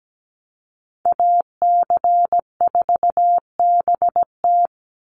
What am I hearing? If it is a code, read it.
Morse code, heard as a keyed tone, about 17 wpm, 700 Hz. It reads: AC4BT
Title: Call Signs - 17wpm